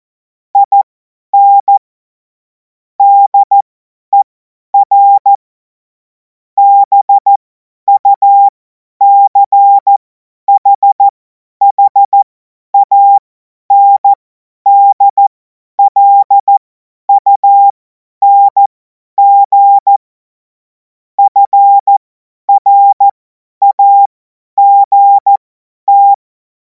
Morse code, heard as a keyed tone, about 14 wpm, spaced as for 10 wpm, 800 Hz. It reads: IN DER BUCHHANDLUNG FRAGT
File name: Morsekurs / Witze / Witz 03. Februar 2026 50 F70